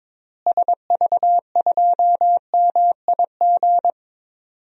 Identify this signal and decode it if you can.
Morse code, heard as a keyed tone, about 22 wpm, 700 Hz. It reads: SV2MIG
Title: DXCC-100 Call Signs - 6 Characters - Repeated 3x - 22wpm